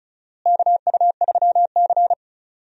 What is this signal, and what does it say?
Morse code, heard as a keyed tone, about 36 wpm, 700 Hz. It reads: KU3C
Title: USA Call Signs - 4 Characters - Repeated 3x - 35wpm